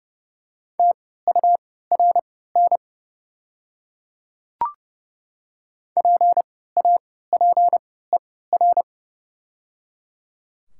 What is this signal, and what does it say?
Morse code, heard as a keyed tone, about 30 wpm, spaced as for 17 wpm, 700 Hz. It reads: TURN PAPER
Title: 3x Character Spacing - Top 300 Words - 30wpm